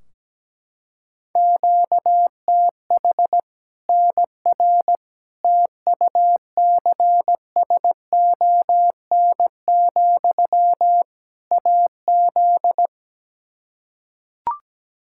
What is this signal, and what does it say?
Morse code, heard as a keyed tone, about 17 wpm, 700 Hz. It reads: QTH NR TUCSON, AZ
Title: K1USN SST QSO Practice 17wpm